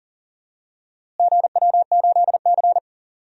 Morse code, heard as a keyed tone, about 37 wpm, 700 Hz. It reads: GW8C
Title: DXCC-100 Call Signs - 4 Characters - Repeated 2x - 40wpm